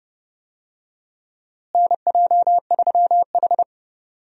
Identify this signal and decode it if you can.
Morse code, heard as a keyed tone, about 31 wpm, 700 Hz. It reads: NJ3H
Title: USA Call Signs - 4 Characters - Repeated 3x - 30wpm